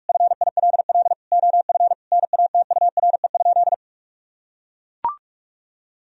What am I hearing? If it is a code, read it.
Morse code, heard as a keyed tone, about 41 wpm, 700 Hz. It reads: FILL OF NATURE?
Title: Sentences from Top 1000 Words - Review Part 3 - 45wpm